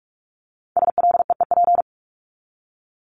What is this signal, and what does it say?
Morse code, heard as a keyed tone, about 44 wpm, 700 Hz. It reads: SLEEP
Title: Top 500 Words 45wpm